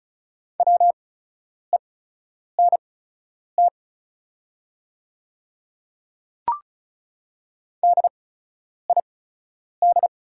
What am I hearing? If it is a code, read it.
Morse code, heard as a keyed tone, about 36 wpm, spaced as for 10 wpm, 690 Hz. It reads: WENT DID
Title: 8x Character Spacing - Top 300 Words - 35wpm